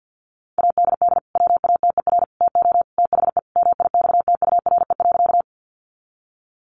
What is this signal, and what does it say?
Morse code, heard as a keyed tone, about 50 wpm, 700 Hz. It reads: ADD WATER TO THE MIXTURE.
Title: Sentences from Top 1500 Words 50wpm